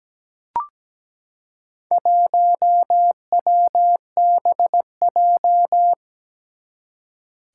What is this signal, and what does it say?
Morse code, heard as a keyed tone, about 17 wpm, 700 Hz. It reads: 1WBJ
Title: Four Challenging Letter-Number Combinations 17wpm